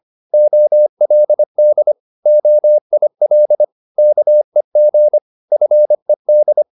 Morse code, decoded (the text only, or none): OLD OIL KEG FED